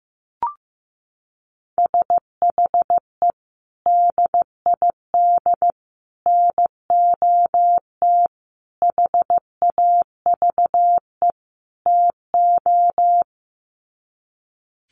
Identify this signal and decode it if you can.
Morse code, heard as a keyed tone, about 15 wpm, 700 Hz. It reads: SHE DID NOT HAVE TO
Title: Sets of 5 Words Encore2 15wpm